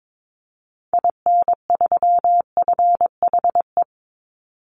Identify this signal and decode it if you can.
Morse code, heard as a keyed tone, about 22 wpm, 700 Hz. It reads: IN3FHE
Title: Call Signs Repeated 2x - 22wpm